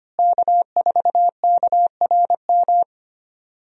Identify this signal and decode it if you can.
Morse code, heard as a keyed tone, about 25 wpm, 700 Hz. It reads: K4KRM
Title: USA Call Signs - 5 Characters - Repeated 3x - 25wpm